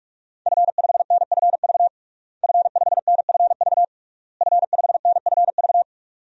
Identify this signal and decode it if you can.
Morse code, heard as a keyed tone, about 42 wpm, 700 Hz. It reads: F5NFV F5NFV F5NFV
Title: DXCC-100 Call Signs - 5 Characters - Repeated 3x - 45wpm